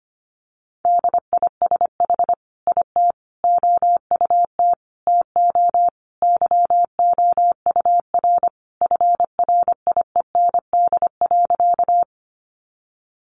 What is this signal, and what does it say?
Morse code, heard as a keyed tone, about 25 wpm, 700 Hz. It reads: DISH IT OUT TO YOUR FRIEND.
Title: Sentences from Top 1300 Words 25wpm